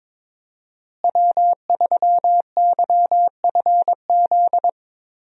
Morse code, heard as a keyed tone, about 22 wpm, 700 Hz. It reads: W3YFZ
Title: USA Call Signs - 5 Characters - Repeated 3x - 22wpm